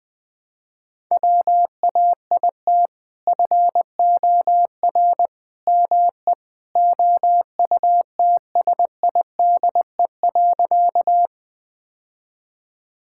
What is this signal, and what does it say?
Morse code, heard as a keyed tone, about 20 wpm, 700 Hz. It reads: WAIT FOR ME OUTSIDE.
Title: Sentences from Top 1000 Words - Review Part 6 - 20wpm